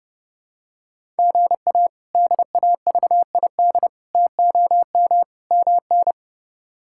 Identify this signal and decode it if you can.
Morse code, heard as a keyed tone, about 32 wpm, 700 Hz. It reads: GA DAVID TOM MN
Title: K1USN SST QSO Practice - 30wpm